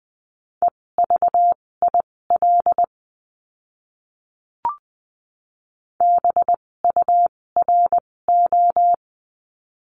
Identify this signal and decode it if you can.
Morse code, heard as a keyed tone, about 20 wpm, spaced as for 16 wpm, 700 Hz. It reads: EVIL BURO
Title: 256 Lesson 42a - Words 12-20wpm